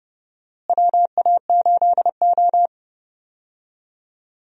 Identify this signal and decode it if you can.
Morse code, heard as a keyed tone, about 32 wpm, 700 Hz. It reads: WA8O
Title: USA Call Signs - 4 Characters - Repeated 3x - 30wpm